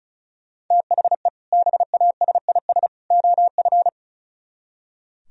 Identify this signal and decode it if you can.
Morse code, heard as a keyed tone, about 42 wpm, 700 Hz. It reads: THE BASIS OF
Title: Speed-Racing - Sets of 3 Words 35wpm